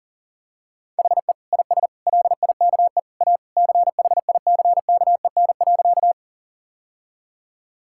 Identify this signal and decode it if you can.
Morse code, heard as a keyed tone, about 40 wpm, 700 Hz. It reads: HE IS LIKE A CHICKEN.
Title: Sentences from Top 1000 Words - Review Part 5 - 40wpm